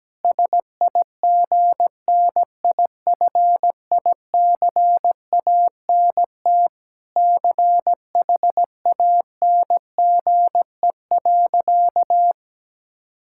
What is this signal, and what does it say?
Morse code, heard as a keyed tone, about 17 wpm, 700 Hz. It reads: SIGNIFICANT CHANGE.
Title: Sentences from Top 1000 Words - Review Part 6 - 17wpm